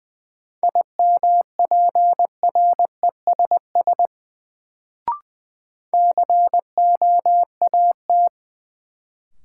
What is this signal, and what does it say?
Morse code, heard as a keyed tone, about 20 wpm, 700 Hz. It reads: IMPRESS COAT
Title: Sentences from Top 1600 Words - New Word Drill Alt - 20wpm